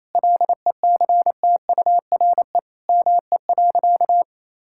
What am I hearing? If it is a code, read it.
Morse code, heard as a keyed tone, about 28 wpm, 700 Hz. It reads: LECTURE ME.
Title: Sentences from Top 1600 Words 28wpm